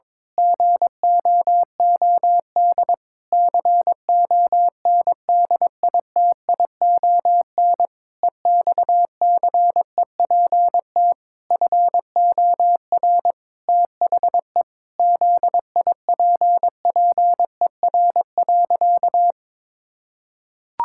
Morse code, heard as a keyed tone, about 22 wpm, 700 Hz. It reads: GOOD CONDITION EXCEPT FOR THE ZIPPER.
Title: Pangram Challenge v2 - 22wpm